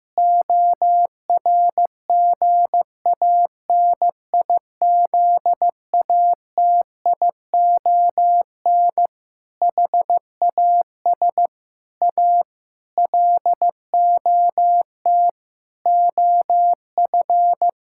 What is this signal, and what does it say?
Morse code, heard as a keyed tone, about 15 wpm, 700 Hz. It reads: ORGANIZATION HAS A LOT OF
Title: Sentences from Top 1000 Words - Review Part 6 - 15wpm